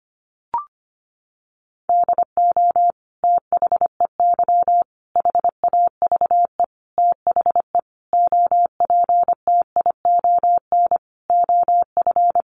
Morse code, heard as a keyed tone, about 25 wpm, 700 Hz. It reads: DO THEY HAVE THE OPTION OF
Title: Sentences from Top 1300 Words 25wpm